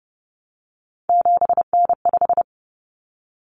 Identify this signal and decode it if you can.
Morse code, heard as a keyed tone, about 30 wpm, 700 Hz. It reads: 7N5
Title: Common Cut Numbers - 3-Digit 30wpm